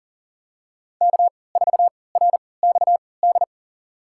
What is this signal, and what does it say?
Morse code, heard as a keyed tone, about 39 wpm, spaced as for 23 wpm, 700 Hz. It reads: K4RXD
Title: USA Call Signs - 5 Characters - 3x Character Spacing - 40wpm